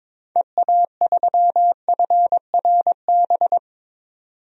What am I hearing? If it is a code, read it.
Morse code, heard as a keyed tone, about 22 wpm, 700 Hz. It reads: EA3FRB
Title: DXCC-100 Call Signs - 6 Characters - 22wpm